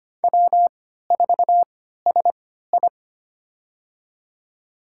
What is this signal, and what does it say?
Morse code, heard as a keyed tone, about 25 wpm, spaced as for 14 wpm, 700 Hz. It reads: W4SI
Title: USA Call Signs - 3x Character Spacing - 25wpm